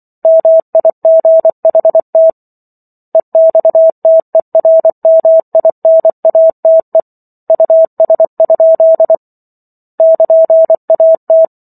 Morse code, heard as a keyed tone, about 24 wpm, 650 Hz. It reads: MIGHT EXTERMINATE US? <KN>AT